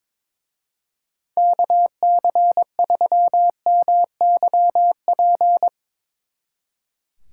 Morse code, heard as a keyed tone, about 24 wpm, 700 Hz. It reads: KC3MYP
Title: Speed Racing - USA 2x3 Call Signs 22wpm